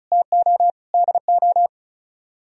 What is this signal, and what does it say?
Morse code, heard as a keyed tone, about 46 wpm, 700 Hz. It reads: TO DO